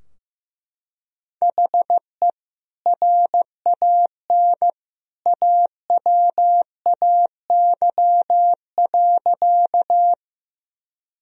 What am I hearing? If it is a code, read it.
Morse code, heard as a keyed tone, about 15 wpm, 700 Hz. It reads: HE RAN AWAY.
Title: Sentences from Top 1000 Words - Review Part 3 - 15wpm